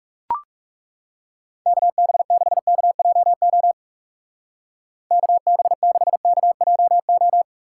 Morse code, heard as a keyed tone, about 41 wpm, 700 Hz. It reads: KB6KJO KB6KJO